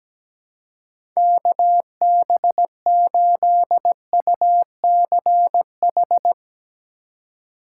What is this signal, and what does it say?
Morse code, heard as a keyed tone, about 17 wpm, 700 Hz. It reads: KB8UCH
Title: USA Call Signs - 6 Characters - 3x Character Spacing - 17wpm